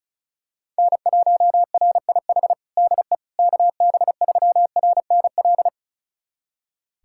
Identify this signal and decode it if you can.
Morse code, heard as a keyed tone, about 34 wpm, 700 Hz. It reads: N1RIH DE KB3RNL
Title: USA Callsign DE Callsign 35wpm